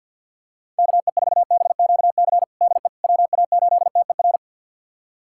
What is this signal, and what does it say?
Morse code, heard as a keyed tone, about 52 wpm, 700 Hz. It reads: KE4BQC DE WA8TER
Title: USA Callsign DE Callsign 50wpm